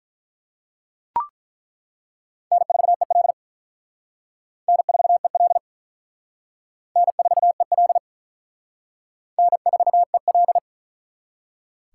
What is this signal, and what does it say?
Morse code, heard as a keyed tone, about 48 wpm, 700 Hz. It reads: N4EL N4EL N4EL N4EL